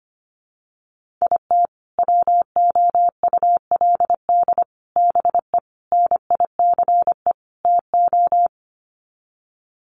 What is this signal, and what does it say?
Morse code, heard as a keyed tone, about 25 wpm, 700 Hz. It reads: IT WOULD BE NICE TO
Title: Sets of 5 Words Encore2 25wpm